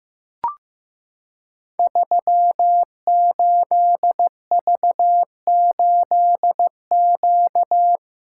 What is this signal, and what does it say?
Morse code, heard as a keyed tone, about 15 wpm, 700 Hz. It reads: 38V8Q
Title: Code Groups - 5 Letter:Numbers - 15wpm